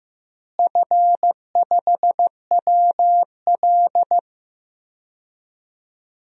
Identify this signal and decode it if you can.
Morse code, heard as a keyed tone, about 15 wpm, 700 Hz. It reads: F5WL
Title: DXCC-100 Call Signs - 4 Characters - Repeated 3x - 15wpm